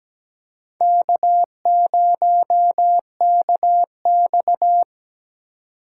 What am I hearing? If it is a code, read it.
Morse code, heard as a keyed tone, about 17 wpm, 700 Hz. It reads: K0KX
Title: K1USN SST QSO Practice - 17wpm